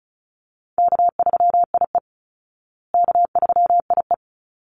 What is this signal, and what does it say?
Morse code, heard as a keyed tone, about 34 wpm, 700 Hz. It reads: K3IE K3IE
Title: Call Signs Repeated 3x - 35wpm